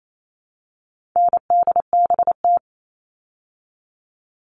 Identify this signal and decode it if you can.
Morse code, heard as a keyed tone, about 28 wpm, 700 Hz. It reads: NDBT
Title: All Cut Numbers - 4-Digit 28wpm